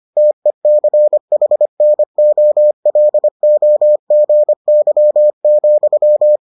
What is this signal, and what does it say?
Morse code, heard as a keyed tone, about 25 wpm, 600 Hz. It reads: TECHNOLOGY,